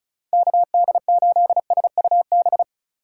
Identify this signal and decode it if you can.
Morse code, heard as a keyed tone, about 44 wpm, 700 Hz. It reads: KD8SUB
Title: Speed Racing - USA Call Signs 35wpm